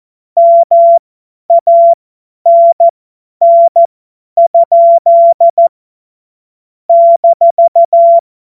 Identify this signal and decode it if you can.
Morse code, heard as a keyed tone, about 14 wpm, spaced as for 10 wpm, 680 Hz. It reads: MANN? -
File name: Morsekurs / Witze / Witz 01. November 2022 50 F70